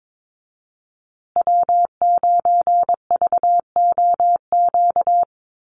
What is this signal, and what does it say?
Morse code, heard as a keyed tone, about 22 wpm, 700 Hz. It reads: W9VOQ
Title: USA Call Signs Repeated 3x 22wpm